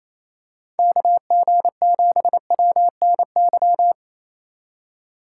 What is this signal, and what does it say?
Morse code, heard as a keyed tone, about 28 wpm, 700 Hz. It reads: KG7WNY
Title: USA Call Signs - 6 Characters - 2x Character Spacing - 28wpm